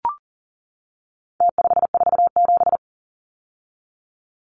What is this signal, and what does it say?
Morse code, heard as a keyed tone, about 40 wpm, 700 Hz. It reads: T547
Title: Numbers - Contest Style - 4-Digit 40wpm